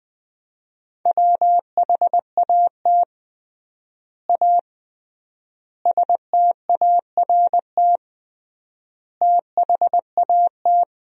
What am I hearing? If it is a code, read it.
Morse code, heard as a keyed tone, about 20 wpm, 700 Hz. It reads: WHAT A START THAT
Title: 3x Word-Spacing - Sentences from Top 300 Words - 20wpm